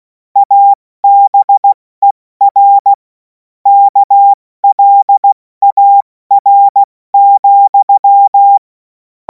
Morse code, heard as a keyed tone, about 16 wpm, spaced as for 14 wpm, 800 Hz. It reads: ABER KLAR,
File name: Morsekurs / Witze / Witz 23. Juni 2026 70 F80